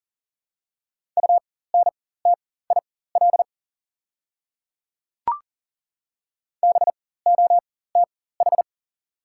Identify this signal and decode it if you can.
Morse code, heard as a keyed tone, about 40 wpm, spaced as for 19 wpm, 700 Hz. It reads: UNTIL BOTH
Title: 4x Character Spacing - Top 300 Words - 40wpm